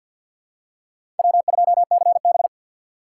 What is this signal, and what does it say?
Morse code, heard as a keyed tone, about 53 wpm, 700 Hz. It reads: W2CB
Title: USA Call Signs - 4 Characters - 6x Character Spacing - 50wpm